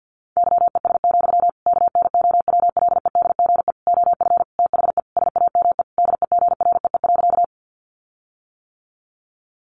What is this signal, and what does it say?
Morse code, heard as a keyed tone, about 48 wpm, 700 Hz. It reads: YES, KNOWLEDGE OF THE SAME DEGREE.
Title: Sentences from Top 1500 Words 50wpm